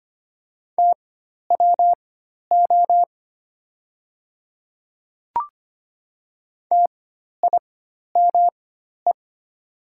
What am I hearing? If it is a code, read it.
Morse code, heard as a keyed tone, about 25 wpm, spaced as for 12 wpm, 700 Hz. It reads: TWO TIME
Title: 4x Character Spacing - Top 100 Words - 25wpm